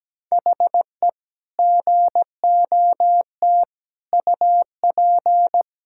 Morse code, heard as a keyed tone, about 17 wpm, 700 Hz. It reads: HE GOT UP
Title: Sentences from Top 1600 Words 17wpm